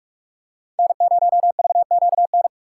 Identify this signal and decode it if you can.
Morse code, heard as a keyed tone, about 42 wpm, 700 Hz. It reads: N0VQN